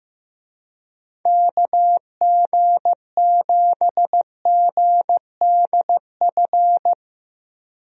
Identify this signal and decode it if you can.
Morse code, heard as a keyed tone, about 15 wpm, 700 Hz. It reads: KG7GDF
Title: USA Call Signs - 6 Characters - Repeated 3x - 15wpm